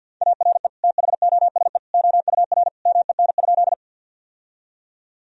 Morse code, heard as a keyed tone, about 49 wpm, 700 Hz. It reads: ARE THOSE OUR MEN?
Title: Sentences from Top 1000 Words - Review Part 1 - 50wpm